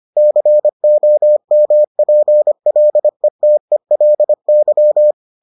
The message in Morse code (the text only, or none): COMPLETELY